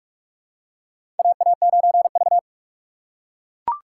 Morse code, heard as a keyed tone, about 43 wpm, 700 Hz. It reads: AA9V
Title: USA Call Signs - 4 Characters - 6x Character Spacing - 45wpm